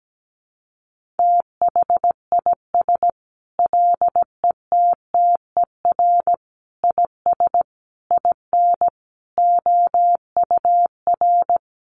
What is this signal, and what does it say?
Morse code, heard as a keyed tone, about 17 wpm, 700 Hz. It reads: THIS LETTER IS IN OUR
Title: Sentences from Top 1400 Words 17wpm